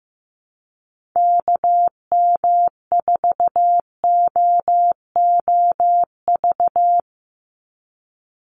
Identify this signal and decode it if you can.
Morse code, heard as a keyed tone, about 15 wpm, 700 Hz. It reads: KM4OOV
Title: USA Call Signs 15wpm